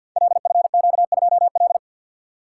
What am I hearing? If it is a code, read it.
Morse code, heard as a keyed tone, about 53 wpm, 700 Hz. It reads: LFQ2L
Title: Code Groups - 5 Letter:Numbers - 50wpm